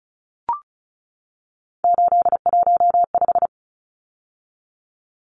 Morse code, heard as a keyed tone, about 34 wpm, 700 Hz. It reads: 815
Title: Numbers - 3-Digit 35wpm